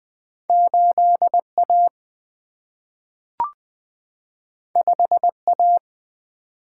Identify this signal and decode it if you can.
Morse code, heard as a keyed tone, about 20 wpm, 700 Hz. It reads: 8 A 5 A
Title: Single-Letter Single-Number 20wpm